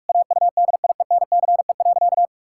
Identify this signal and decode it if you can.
Morse code, heard as a keyed tone, about 41 wpm, 700 Hz. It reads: AUDIENCE.
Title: Sentences from Top 1000 Words - Review Part 5 - 45wpm